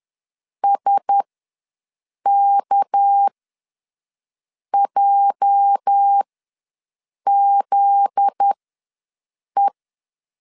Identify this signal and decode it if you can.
Morse code, heard as a keyed tone, about 10 wpm, spaced as for 6 wpm, 800 Hz. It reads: SKJZE
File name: LEZIONI / 8 YJ45 / 8 YJ ws 10wpm 4043